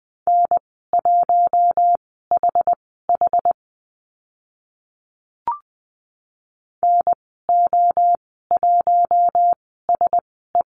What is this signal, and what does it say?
Morse code, heard as a keyed tone, about 20 wpm, spaced as for 14 wpm, 700 Hz. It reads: N1HH NO1SE
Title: 062 Lesson 11g - Letter h - Call Signs 20wpm